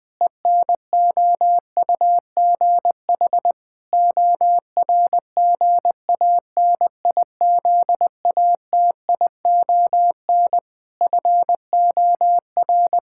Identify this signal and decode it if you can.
Morse code, heard as a keyed tone, about 20 wpm, 700 Hz. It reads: ENOUGH ORGANIZATION FOR
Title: Sentences from Top 1000 Words - Review Part 5 - 20wpm